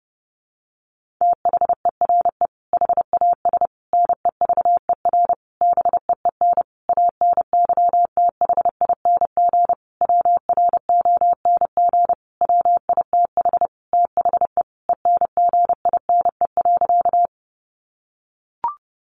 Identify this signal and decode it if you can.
Morse code, heard as a keyed tone, about 28 wpm, 700 Hz. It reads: THERE HAS NEVER BEEN ANYTHING WRONG WITH THE ENGINE.
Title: Sentences from Top 1500 Words 30wpm